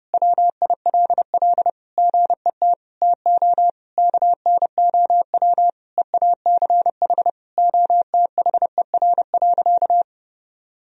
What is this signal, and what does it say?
Morse code, heard as a keyed tone, about 30 wpm, 700 Hz. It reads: WILL GET TO KNOW EACH OTHER.
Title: Sentences from Top 1000 Words - Review Part 3 - 30wpm